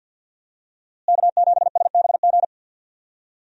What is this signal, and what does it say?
Morse code, heard as a keyed tone, about 48 wpm, 700 Hz. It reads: K7SBG
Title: USA Call Signs - 5 Characters - Repeated 2x - 50wpm